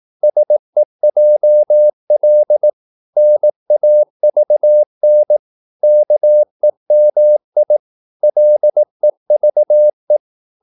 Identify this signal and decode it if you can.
Morse code, heard as a keyed tone, about 18 wpm, 600 Hz. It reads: SEJL NAVN KEMI LEVE
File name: Korte ord DK 18wpm